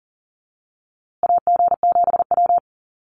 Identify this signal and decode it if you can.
Morse code, heard as a keyed tone, about 40 wpm, 700 Hz. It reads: AG7W
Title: USA Call Signs Repeated 3x 40wpm